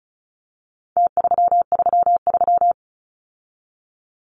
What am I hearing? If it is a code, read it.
Morse code, heard as a keyed tone, about 34 wpm, 700 Hz. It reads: T333
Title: Numbers - Contest Style - 4-Digit 35wpm